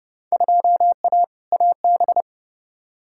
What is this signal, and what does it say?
Morse code, heard as a keyed tone, about 30 wpm, 700 Hz. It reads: 2A AB
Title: Field Day Exchanges - 30wpm